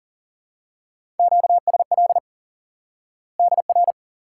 Morse code, heard as a keyed tone, about 40 wpm, 700 Hz. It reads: QSL DR
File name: IQE Course 3 - Lesson 89 - YAGI - Two Elements - 40wpm